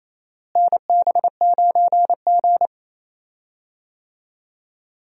Morse code, heard as a keyed tone, about 28 wpm, 700 Hz. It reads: NB9G
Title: USA Call Signs - 4 Characters - Repeated 3x - 28wpm